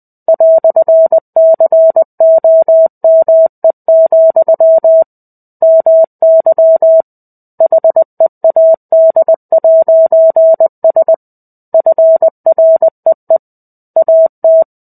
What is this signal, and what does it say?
Morse code, heard as a keyed tone, about 20 wpm, 650 Hz. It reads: "COME, MY HEAD'S FREE AT
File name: 05-Chapter-05 20 WPM